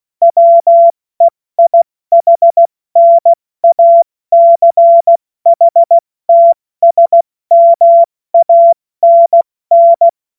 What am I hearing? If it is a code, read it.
Morse code, heard as a keyed tone, about 16 wpm, 680 Hz. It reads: WEIHNACHTSMANN
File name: Morsekurs / Witze / Witz 03. Januar 2023 70 F80